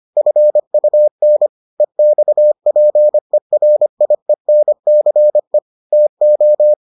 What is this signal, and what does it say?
Morse code, heard as a keyed tone, about 25 wpm, 600 Hz. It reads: FUN EXPERIENCE TO